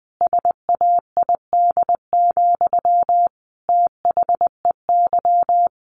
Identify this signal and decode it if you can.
Morse code, heard as a keyed tone, about 20 wpm, 700 Hz. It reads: SAID, THEY
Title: Sentences from Top 700 Words 20wpm